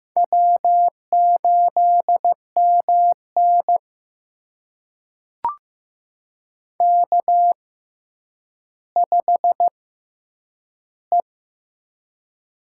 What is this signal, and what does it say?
Morse code, heard as a keyed tone, about 15 wpm, spaced as for 9 wpm, 700 Hz. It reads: W8MN K5E
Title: USA Call Signs - 4 Characters - 6x Character Spacing - 15wpm